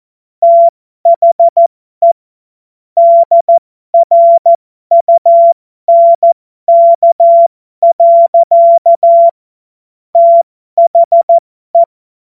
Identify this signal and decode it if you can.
Morse code, heard as a keyed tone, about 14 wpm, spaced as for 12 wpm, 680 Hz. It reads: THE DRUNK. THE